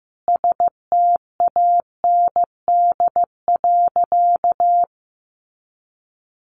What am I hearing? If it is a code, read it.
Morse code, heard as a keyed tone, about 15 wpm, 700 Hz. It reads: STAND.
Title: Additional Easy Phrases 15wpm